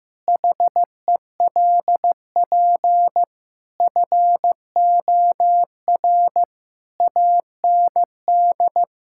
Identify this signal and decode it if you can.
Morse code, heard as a keyed tone, about 15 wpm, 700 Hz. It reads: HELP FOR AND
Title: Pangram Challenge v2 - 15wpm